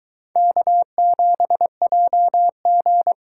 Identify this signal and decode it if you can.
Morse code, heard as a keyed tone, about 22 wpm, 700 Hz. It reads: K7JG